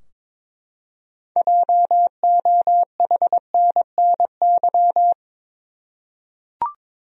Morse code, heard as a keyed tone, about 22 wpm, 700 Hz. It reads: JOHNNY
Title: USA Common Names - 22wpm